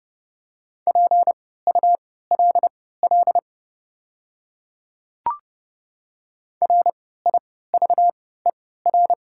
Morse code, heard as a keyed tone, about 30 wpm, spaced as for 17 wpm, 700 Hz. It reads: PULL RIVER
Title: 3x Character Spacing - Top 500 Words - 30wpm